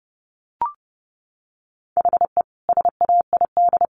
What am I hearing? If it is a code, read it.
Morse code, heard as a keyed tone, about 30 wpm, 700 Hz. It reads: HE SAID
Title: Sentences from Top 1500 Words 30wpm